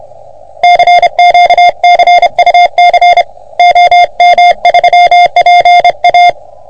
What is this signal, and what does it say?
Morse code, heard as a keyed tone, about 30 wpm, 700 Hz. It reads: CQ CUC OM3PA
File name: OM3PA PILUP CUC 388